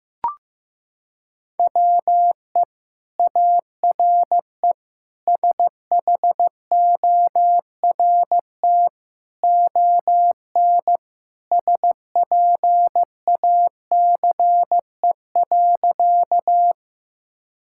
Morse code, heard as a keyed tone, about 15 wpm, 700 Hz. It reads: WE ARE SHORT ON SPACE.
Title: Sentences from Top 1000 Words - Review Part 5 - 15wpm